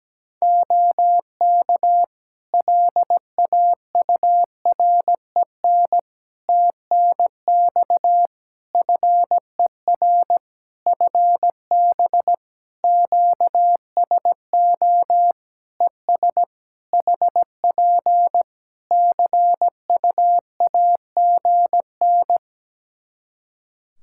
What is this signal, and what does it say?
Morse code, heard as a keyed tone, about 17 wpm, 700 Hz. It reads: OK LAUREN TNX FER FB QSO ES HP CUAGN
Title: Practice QSO - Exchange 4 - Ending - 17wpm